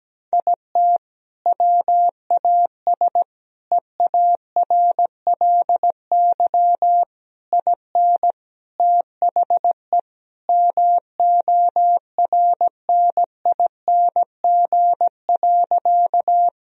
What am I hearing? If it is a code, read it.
Morse code, heard as a keyed tone, about 17 wpm, 700 Hz. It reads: IT WAS EARLY IN THE MORNING.
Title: Sentences from Top 1000 Words - Review Part 2 - 17wpm